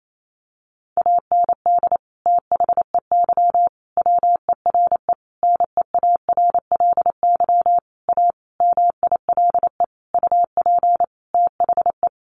Morse code, heard as a keyed tone, about 28 wpm, 700 Hz. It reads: AND THEY WERE NEARLY A MILE UP THE
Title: Sentences from Top 1300 Words 28wpm